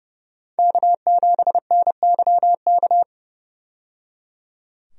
Morse code, heard as a keyed tone, about 30 wpm, 700 Hz. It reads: K7NYK
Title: USA Call Signs - 5 Characters - 30wpm